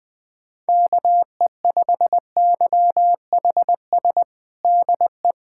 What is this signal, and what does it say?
Morse code, heard as a keyed tone, about 21 wpm, 700 Hz. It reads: KE5YHS DE
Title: USA Callsign DE Callsign - 6 Characters - 20wpm